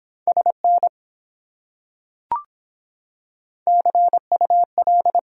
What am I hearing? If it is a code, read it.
Morse code, heard as a keyed tone, about 24 wpm, 700 Hz. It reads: SN CUL